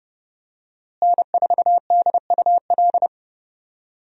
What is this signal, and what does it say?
Morse code, heard as a keyed tone, about 30 wpm, 700 Hz. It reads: N4DUL
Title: USA Call Signs - 5 Characters - Repeated 3x - 30wpm